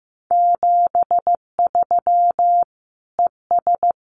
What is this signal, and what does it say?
Morse code, heard as a keyed tone, about 15 wpm, 700 Hz. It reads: 73 ES
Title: Common QSO Phrases 15wpm